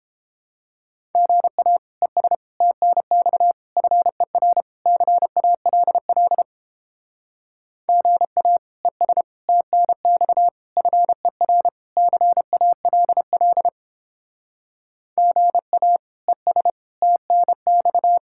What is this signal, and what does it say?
Morse code, heard as a keyed tone, about 31 wpm, 700 Hz. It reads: GA ES TNX FER CALL GA ES TNX FER CALL GA ES TNX